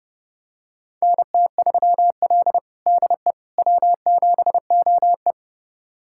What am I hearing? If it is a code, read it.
Morse code, heard as a keyed tone, about 31 wpm, 700 Hz. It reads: NT3L DE W7OE
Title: USA Callsign DE Callsign - 4 Characters - 30wpm